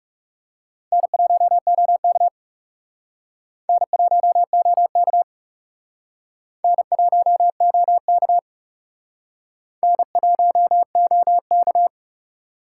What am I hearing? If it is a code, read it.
Morse code, heard as a keyed tone, about 37 wpm, 700 Hz. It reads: N1OK N1OK N1OK N1OK
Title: Speed Racing - USA 1x2 Call Signs 30wpm